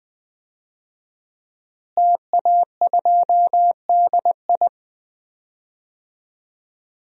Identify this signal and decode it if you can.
Morse code, heard as a keyed tone, about 20 wpm, 700 Hz. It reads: TA2DI
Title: DXCC-100 Call Signs - 5 Characters - Repeated 2x - 20wpm